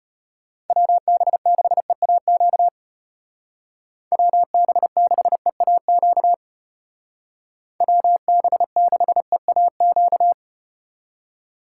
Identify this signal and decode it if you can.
Morse code, heard as a keyed tone, about 33 wpm, 700 Hz. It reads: WB6EAQ WB6EAQ WB6EAQ